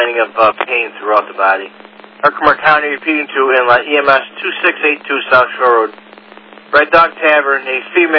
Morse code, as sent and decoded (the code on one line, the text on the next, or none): none